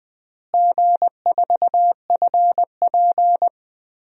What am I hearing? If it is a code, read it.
Morse code, heard as a keyed tone, about 20 wpm, 700 Hz. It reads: G4FP
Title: DXCC-100 Call Signs - 4 Characters - Repeated 3x - 20wpm